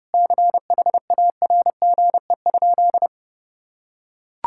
Morse code, heard as a keyed tone, about 30 wpm, 700 Hz. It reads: CHARGE?
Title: Sentences from Top 1000 Words - Review Part 6 - 30wpm